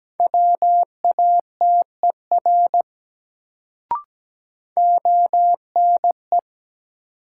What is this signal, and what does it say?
Morse code, heard as a keyed tone, about 17 wpm, 700 Hz. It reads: WATER ONE
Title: Sentences from Top 100 Words - New Word Drill - 17wpm